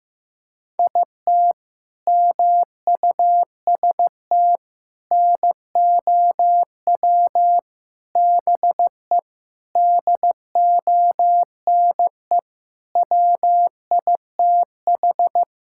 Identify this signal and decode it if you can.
Morse code, heard as a keyed tone, about 15 wpm, 700 Hz. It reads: IT MUST NOW BE DONE WITH
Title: Sentences from Top 1000 Words - Review Part 5 - 15wpm